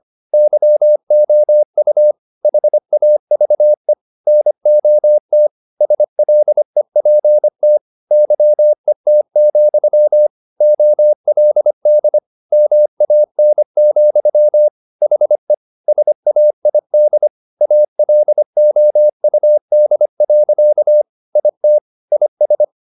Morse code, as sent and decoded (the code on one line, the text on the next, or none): -.-- --- ..- / .... .- ...- . / -. --- - / ... .-.. .--. - / -.-- . - --..-- / --- .-.. -.. / -- .- -. --..-- / .... / ... .- .. -.. / .- .-.. --- ..- -.. .-.-.- / .. - / .. ...
YOU HAVE NOT SLEPT YET, OLD MAN, HE SAID ALOUD. IT IS